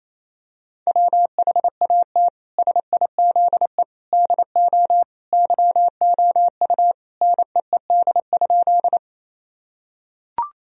Morse code, heard as a keyed tone, about 28 wpm, 700 Hz. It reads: WHAT SIZE DO YOU NEED?
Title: Sentences from Top 1000 Words - Review Part 2 - 28wpm